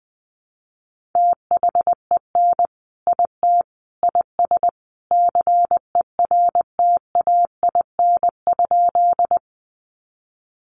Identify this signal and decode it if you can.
Morse code, heard as a keyed tone, about 20 wpm, 700 Hz. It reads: THEN IT IS CERTAIN?
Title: Sentences from Top 500 Words 20wpm